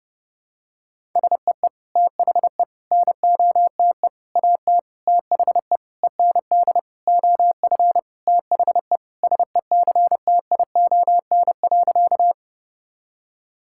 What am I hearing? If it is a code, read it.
Morse code, heard as a keyed tone, about 30 wpm, 700 Hz. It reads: SEE THE NOTE AT THE END OF THE SECTION.
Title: Sentences from Top 1000 Words - Review Part 5 - 30wpm